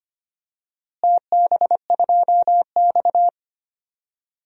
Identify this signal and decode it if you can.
Morse code, heard as a keyed tone, about 25 wpm, 700 Hz. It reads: TB2X
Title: DXCC-100 Call Signs - 4 Characters - Repeated 2x - 25wpm